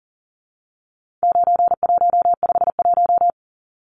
Morse code, heard as a keyed tone, about 40 wpm, 700 Hz. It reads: 9151
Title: Numbers - 4-Digit 40wpm